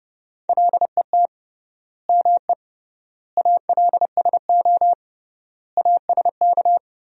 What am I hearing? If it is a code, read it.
Morse code, heard as a keyed tone, about 30 wpm, 700 Hz. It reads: LET ME ALSO ASK
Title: 3x Word-Spacing - Sentences from Top 400 Words - 30wpm